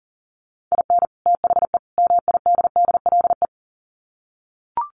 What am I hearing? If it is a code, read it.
Morse code, heard as a keyed tone, about 40 wpm, 700 Hz. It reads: IN THE MIDDLE
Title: Sets of 3 Words 40wpm